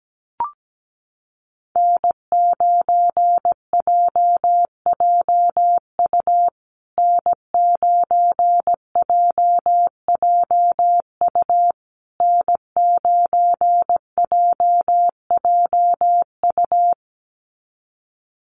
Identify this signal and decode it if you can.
Morse code, heard as a keyed tone, about 17 wpm, 700 Hz. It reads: N9JJU N9JJU N9JJU
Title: USA Call Signs Repeated 3x 17wpm